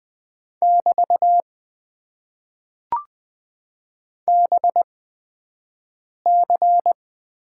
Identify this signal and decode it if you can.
Morse code, heard as a keyed tone, about 20 wpm, spaced as for 5 wpm, 700 Hz. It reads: <BT> BC
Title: Top 90 QSO Elements - 8x Character Spacing - 20wpm